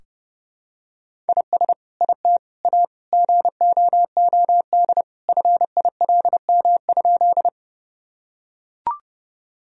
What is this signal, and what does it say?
Morse code, heard as a keyed tone, about 30 wpm, 700 Hz. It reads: IS IT A GOOD FILM?
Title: Sentences from Top 1000 Words - Review Part 3 - 30wpm